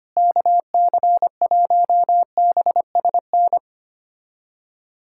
Morse code, heard as a keyed tone, about 25 wpm, 700 Hz. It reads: KC1BSN
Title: USA Callsign DE Callsign 25wpm